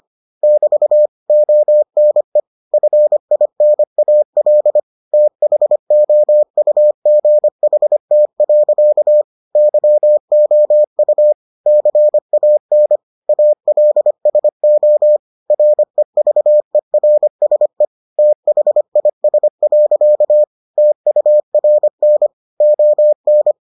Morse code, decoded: <BT> ONE FINAL THOUGHT. YOU CAN ALSO REVERSE THIS. TURN ON

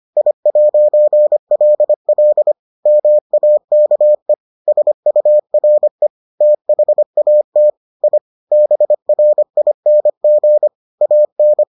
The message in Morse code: I'LL MAKE SURE THAT I BRING AN

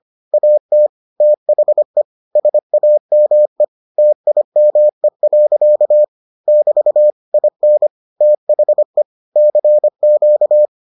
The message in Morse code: AT THE SAME TIME. <BT> IN THE CQ

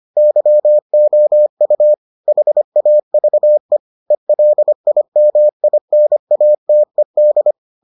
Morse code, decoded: YOU HAVE ELIMINATED

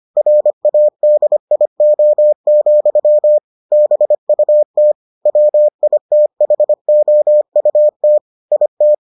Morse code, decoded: RADIO, BUT WITHOUT IT

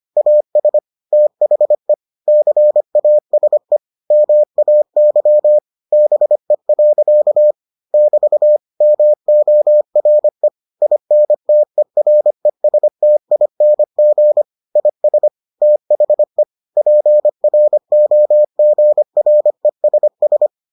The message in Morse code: AS THE CASE MAY BE. <BT> MORE INTERESTING IS THE PROGRESS

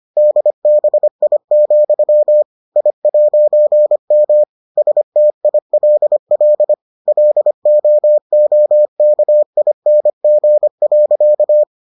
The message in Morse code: DBI, I'M STILL LOOKING.

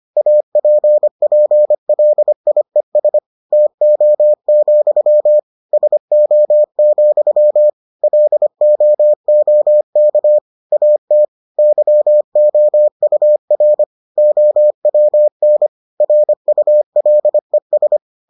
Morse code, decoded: APPLIES TO, SO, LOOK AT YOUR OWN RULES

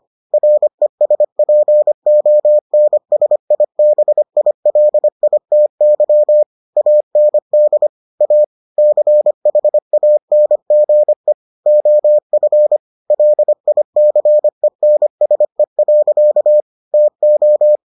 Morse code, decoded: RESPONSIBILITY AND A CHANGE OF LICENSE. TO

